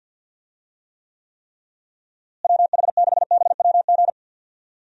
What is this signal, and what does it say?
Morse code, heard as a keyed tone, about 50 wpm, 700 Hz. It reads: WH6BWG
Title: USA Call Signs - 6 Characters - Repeated 3x - 50wpm